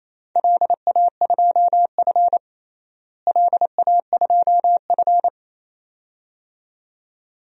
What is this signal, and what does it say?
Morse code, heard as a keyed tone, about 28 wpm, 700 Hz. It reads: LA2F LA2F
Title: DXCC-100 Call Signs - 4 Characters - Repeated 2x - 28wpm